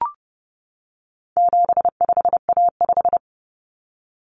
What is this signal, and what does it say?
Morse code, heard as a keyed tone, about 30 wpm, 700 Hz. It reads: 75A5
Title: Common Cut Numbers - 4-Digit 30wpm